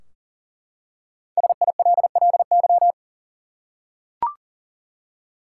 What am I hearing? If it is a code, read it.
Morse code, heard as a keyed tone, about 39 wpm, 700 Hz. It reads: SILLY
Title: Five Letter Words - 40wpm